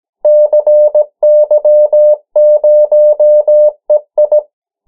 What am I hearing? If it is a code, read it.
Morse code, heard as a keyed tone, about 17 wpm, 600 Hz. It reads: CY0EI
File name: morse mp3 / 14 urok 13